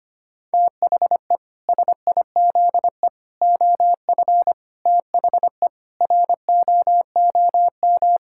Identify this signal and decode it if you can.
Morse code, heard as a keyed tone, about 25 wpm, 700 Hz. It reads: THE SIZE OF THE ROOM
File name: practice / 25wpm / Sentences with 3-Word Phrases - 25wpm